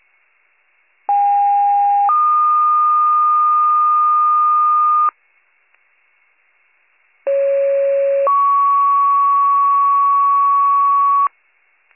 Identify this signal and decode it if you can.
Two-tone page, heard as a one-second tone, then a three-second tone, about 810 then 1220 Hz, about 550 then 1120 Hz. It is heard from 1.1 to 11.3 s.